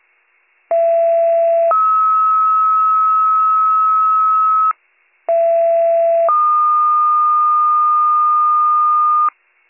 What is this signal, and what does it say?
Two-tone page, heard as a one-second tone, then a three-second tone, about 670 then 1290 Hz, about 670 then 1220 Hz. It is heard from 0.7 to 9.3 s.